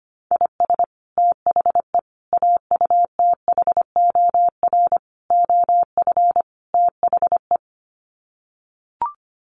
Morse code, heard as a keyed tone, about 25 wpm, 700 Hz. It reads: IS THE AUTHOR OF THE
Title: Sets of 5 Words Encore 25wpm